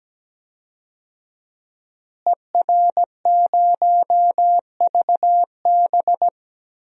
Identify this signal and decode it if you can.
Morse code, heard as a keyed tone, about 17 wpm, 700 Hz. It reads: ER0VB
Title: DXCC-100 Call Signs - 5 Characters - Repeated 3x - 17wpm